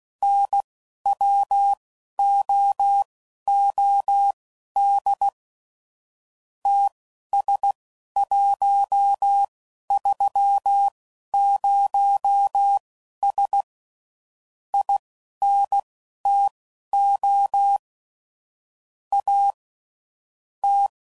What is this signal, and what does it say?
Morse code, heard as a keyed tone, about 16 wpm, spaced as for 11 wpm, 800 Hz. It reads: NWOOD TS130S INTO A T
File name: CW Code Groups @15WPM / QSOs / WD0BGP Spaced